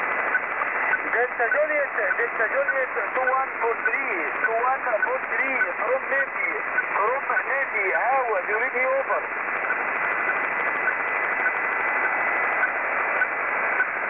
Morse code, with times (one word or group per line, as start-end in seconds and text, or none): none